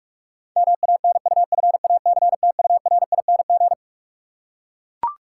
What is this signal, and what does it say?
Morse code, heard as a keyed tone, about 44 wpm, 700 Hz. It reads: MANUFACTURING
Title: Practice QSO - Exchange 3 - More Info - 45wpm